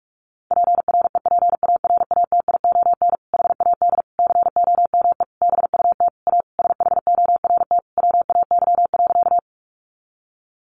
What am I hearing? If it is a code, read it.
Morse code, heard as a keyed tone, about 44 wpm, 700 Hz. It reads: PREPARATION HAD COME BUT A SHORT WAY.
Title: Sentences from Top 1400 Words 45wpm